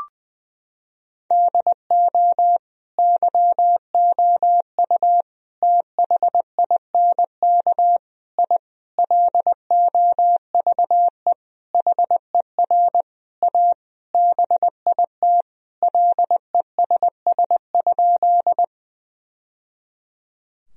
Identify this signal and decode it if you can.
Morse code, heard as a keyed tone, about 21 wpm, 700 Hz. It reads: DO YOU THINK I LOVE HER A BIT LESS?
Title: Sentences from Top 1000 Words - Review Part 2 - 20wpm